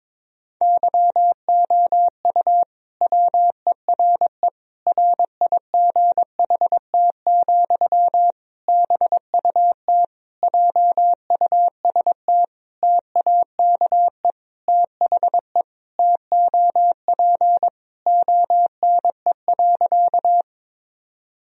Morse code, heard as a keyed tone, about 22 wpm, 700 Hz. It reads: YOU WERE RIGHT, BUT JUST TAKE THE TOP ONE.
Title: Sentences from Top 1000 Words - Review Part 2 - 22wpm